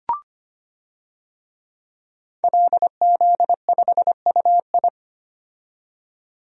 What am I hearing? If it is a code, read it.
Morse code, heard as a keyed tone, about 25 wpm, 700 Hz. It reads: LZ5UI
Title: DXCC-100 Call Signs - 5 Characters - 25wpm